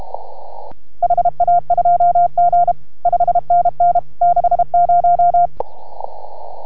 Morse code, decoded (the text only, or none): HA2G 5NN 60